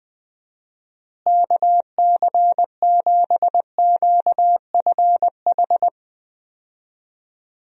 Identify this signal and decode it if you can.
Morse code, heard as a keyed tone, about 20 wpm, 700 Hz. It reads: KC7QFH
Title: USA Call Signs - 6 Characters - Repeated 3x - 20wpm